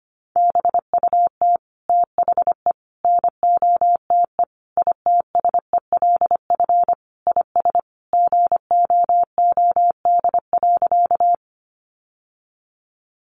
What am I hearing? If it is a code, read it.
Morse code, heard as a keyed tone, about 25 wpm, 700 Hz. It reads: BUT THE NOTE ITSELF IS GOOD.
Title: Sentences from Top 800 Words 25wpm